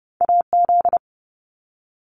Morse code, heard as a keyed tone, about 30 wpm, 700 Hz. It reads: AZ
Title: US States 30wpm